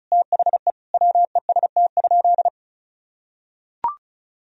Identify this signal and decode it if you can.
Morse code, heard as a keyed tone, about 34 wpm, 700 Hz. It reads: THE WEST?
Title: Sentences from Top 1000 Words - Review Part 2 - 35wpm